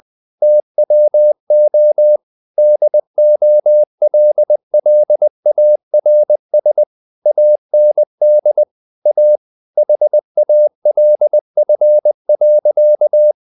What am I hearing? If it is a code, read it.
Morse code, heard as a keyed tone, about 20 wpm, 600 Hz. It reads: TWO DOLLARS AND A HALF.
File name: the old man and the sea / the old man and the sea part 1 20wpm